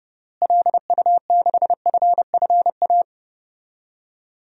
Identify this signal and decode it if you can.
Morse code, heard as a keyed tone, about 30 wpm, 700 Hz. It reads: LU6FFA
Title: DXCC-100 Call Signs - 6 Characters - Repeated 3x - 30wpm